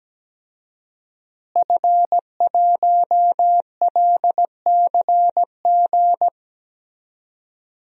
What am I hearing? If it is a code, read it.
Morse code, heard as a keyed tone, about 17 wpm, 700 Hz. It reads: F1LCG
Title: DXCC-100 Call Signs - 5 Characters - Repeated 3x - 17wpm